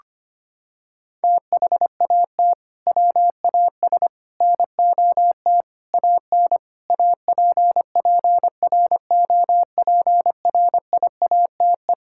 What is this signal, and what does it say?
Morse code, heard as a keyed tone, about 25 wpm, 700 Hz. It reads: THAT WAS NOT AN APPROPRIATE
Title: Sentences from Top 1000 Words - Review Part 5 - 25wpm